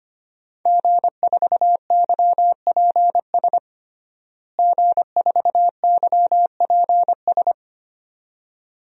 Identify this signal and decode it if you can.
Morse code, heard as a keyed tone, about 25 wpm, 700 Hz. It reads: G4YPS G4YPS
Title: DXCC-100 Call Signs - 5 Characters - Repeated 2x - 25wpm